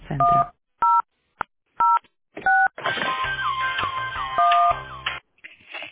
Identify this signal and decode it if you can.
DTMF, heard as telephone touch tones, tones of about 230 ms. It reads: 1006 1